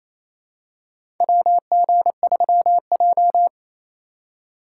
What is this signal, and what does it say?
Morse code, heard as a keyed tone, about 28 wpm, 700 Hz. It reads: WG3J
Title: USA Call Signs - 4 Characters - 2x Character Spacing - 28wpm